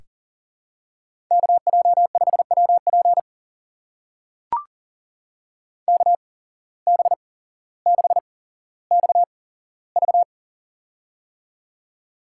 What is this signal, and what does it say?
Morse code, heard as a keyed tone, about 39 wpm, spaced as for 17 wpm, 700 Hz. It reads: KJ5WP KB6XV
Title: USA Call Signs - 5 Characters - 8x Character Spacing - 40wpm